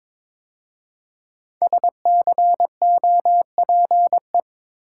Morse code, heard as a keyed tone, about 23 wpm, 700 Hz. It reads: SCOPE
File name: 5 Letter Words - First 3 Letters Given - 22wpm